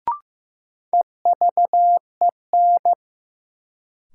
Morse code, heard as a keyed tone, about 15 wpm, 700 Hz. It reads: EVEN
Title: Sentences from Top 600 Words - New Word Drill - 15wpm